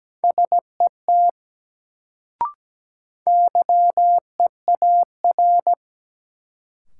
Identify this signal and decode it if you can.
Morse code, heard as a keyed tone, about 17 wpm, 700 Hz. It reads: SET YEAR
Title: Sentences from Top 200 Words - New Word Drill - 17wpm